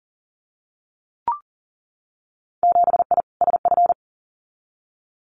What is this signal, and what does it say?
Morse code, heard as a keyed tone, about 40 wpm, 700 Hz. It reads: 7I SF
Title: Winter Field Day Exchanges Encore 40wpm